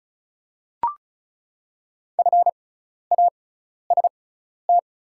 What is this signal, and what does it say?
Morse code, heard as a keyed tone, about 36 wpm, spaced as for 12 wpm, 690 Hz. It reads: FAST
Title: 6x Character Spacing - Top 500 Words - 35wpm